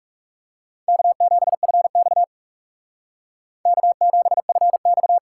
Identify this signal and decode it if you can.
Morse code, heard as a keyed tone, about 36 wpm, 700 Hz. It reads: K7FX K7FX
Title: Speed Racing - USA 1x2 Call Signs 30wpm